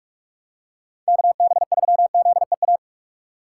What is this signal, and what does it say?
Morse code, heard as a keyed tone, about 44 wpm, 700 Hz. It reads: KB3ZEA
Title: USA Call Signs - 6 Characters - 4x Character Spacing - 45wpm